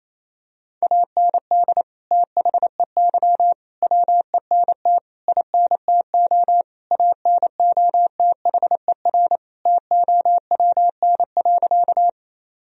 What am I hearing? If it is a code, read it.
Morse code, heard as a keyed tone, about 28 wpm, 700 Hz. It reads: AND THEY WENT INTO ANOTHER TOWN.
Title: Sentences from Top 1000 Words - Review Part 2 - 28wpm